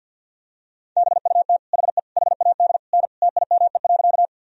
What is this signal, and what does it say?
Morse code, heard as a keyed tone, about 50 wpm, 700 Hz. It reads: BUT HE HAD N TIME.
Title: Sentences from Top 100 Words and Abbreviations 50wpm